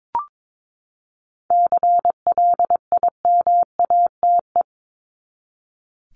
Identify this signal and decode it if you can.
Morse code, heard as a keyed tone, about 22 wpm, 700 Hz. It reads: CLIMATE
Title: Top 1501-1750 Words 22wpm